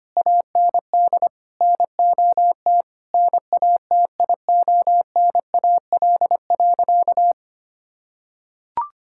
Morse code, heard as a keyed tone, about 25 wpm, 700 Hz. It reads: AND NOT NATIONAL.
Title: Sentences from Top 1000 Words - Review Part 5 - 25wpm